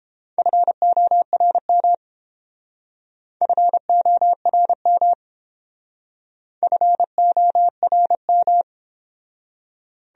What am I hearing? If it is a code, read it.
Morse code, heard as a keyed tone, about 28 wpm, 700 Hz. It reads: FORM FORM FORM